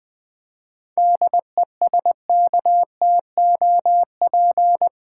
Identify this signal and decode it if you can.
Morse code, heard as a keyed tone, about 20 wpm, 700 Hz. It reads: DESKTOP
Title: 7 Letter Words - First 3 Letters Given - 20wpm